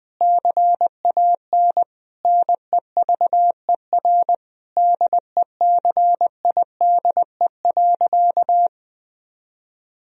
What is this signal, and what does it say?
Morse code, heard as a keyed tone, about 20 wpm, 700 Hz. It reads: CAN NEVER DECIDE.
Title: Sentences from Top 1000 Words - Review Part 3 - 20wpm